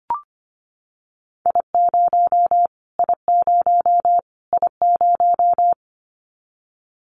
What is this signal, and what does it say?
Morse code, heard as a keyed tone, about 25 wpm, 700 Hz. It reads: I0 I0 I0
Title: Callsign Prefixes for Top 25 DXCC - Repeated 3x - 25wpm